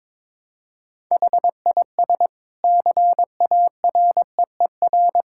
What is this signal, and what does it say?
Morse code, heard as a keyed tone, about 22 wpm, 700 Hz. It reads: HIS CAREER
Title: Sentences from Top 1000 Words - Review Part 6 - 22wpm